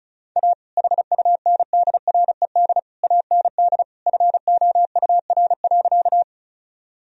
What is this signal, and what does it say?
Morse code, heard as a keyed tone, about 36 wpm, 700 Hz. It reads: A HUNDRED AND FOUR.
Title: Sentences from Top 1000 Words - Review Part 2 - 35wpm